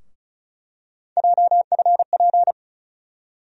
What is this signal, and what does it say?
Morse code, heard as a keyed tone, about 36 wpm, 700 Hz. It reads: JFP
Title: USA Call Sign Postfixes - 3 Letters 35wpm